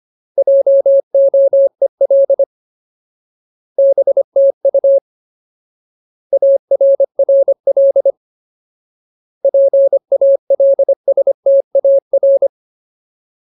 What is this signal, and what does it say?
Morse code, heard as a keyed tone, about 25 wpm, 550 Hz. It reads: JOEL BTU ARRL PALSTAR
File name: Ultimate Word List / 25WPM Ultimate